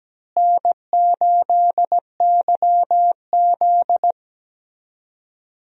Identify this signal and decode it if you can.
Morse code, heard as a keyed tone, about 17 wpm, 700 Hz. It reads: N8YZ
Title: USA Call Signs - 4 Characters - Repeated 3x - 17wpm